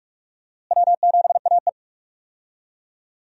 Morse code, heard as a keyed tone, about 46 wpm, 700 Hz. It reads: W7AE
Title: USA Call Signs - 4 Characters - Repeated 2x - 45wpm